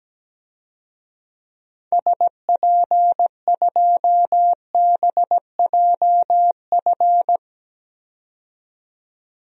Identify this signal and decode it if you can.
Morse code, heard as a keyed tone, about 17 wpm, 700 Hz. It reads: SP2BJF
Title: DXCC-100 Call Signs - 6 Characters - 17wpm